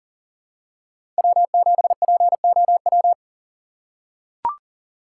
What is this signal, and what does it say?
Morse code, heard as a keyed tone, about 38 wpm, 700 Hz. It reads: W7POW
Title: USA Call Signs - 5 Characters - 2x Character Spacing - 40wpm